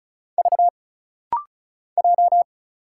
Morse code, heard as a keyed tone, about 34 wpm, 700 Hz. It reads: V J
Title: Single Letter-Number Alt - Rapid-Fire 35wpm